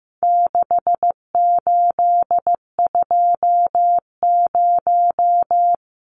Morse code, heard as a keyed tone, about 15 wpm, 700 Hz. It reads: 6820
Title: Numbers - 4-Digit 15wpm